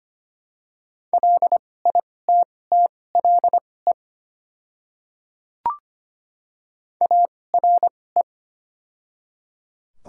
Morse code, heard as a keyed tone, about 25 wpm, spaced as for 18 wpm, 700 Hz. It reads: LITTLE ARE
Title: 2x Character Spacing - Top 200 Words - 25wpm